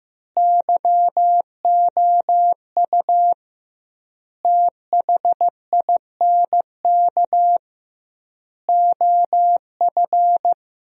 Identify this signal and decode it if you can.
Morse code, heard as a keyed tone, about 15 wpm, 700 Hz. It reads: YOU THINK OF
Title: 2x Word-Spacing - Sentences from Top 200 Words - 15wpm